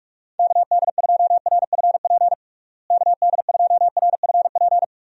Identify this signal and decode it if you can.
Morse code, heard as a keyed tone, about 46 wpm, 700 Hz. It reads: KD2RFP KD2RFP
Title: USA Call Signs - 6 Characters - Repeated 3x - 45wpm